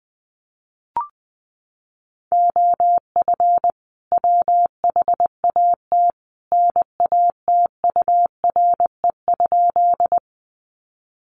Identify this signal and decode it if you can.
Morse code, heard as a keyed tone, about 20 wpm, 700 Hz. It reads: OF WHAT NATURE?
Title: Sentences from Top 600 Words 20wpm